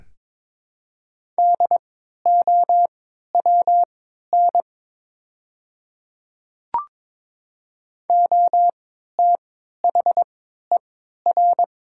Morse code, heard as a keyed tone, about 22 wpm, spaced as for 12 wpm, 700 Hz. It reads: DOWN OTHER
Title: 3x Character Spacing - Top 100 Words - 22wpm